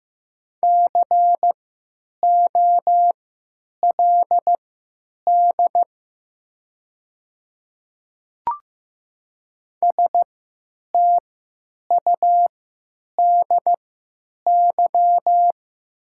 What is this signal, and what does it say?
Morse code, heard as a keyed tone, about 15 wpm, spaced as for 9 wpm, 700 Hz. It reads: COLD STUDY
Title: 3x Character Spacing - Top 500 Words - 15wpm